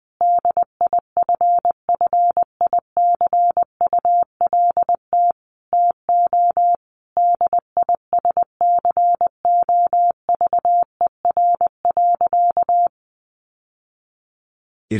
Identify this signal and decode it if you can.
Morse code, heard as a keyed tone, about 20 wpm, 700 Hz. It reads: DIFFICULT TO DISCOVER.
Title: Sentences from Top 1100 Words 20wpm